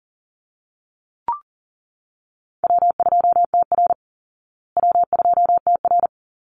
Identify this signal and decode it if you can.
Morse code, heard as a keyed tone, about 40 wpm, 700 Hz. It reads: W2TR W2TR
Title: Easy Call Signs Repeated 2x - 40wpm